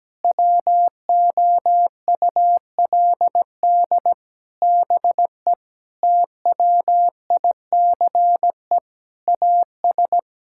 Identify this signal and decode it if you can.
Morse code, heard as a keyed tone, about 17 wpm, 700 Hz. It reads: WOULD BE TWICE AS
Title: Sentences from Top 1600 Words 17wpm